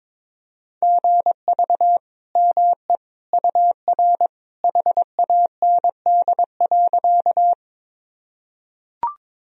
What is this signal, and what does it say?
Morse code, heard as a keyed tone, about 22 wpm, 700 Hz. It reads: GV ME UR HAND.
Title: Sentences from Top 200 Words and Abbreviations 22wpm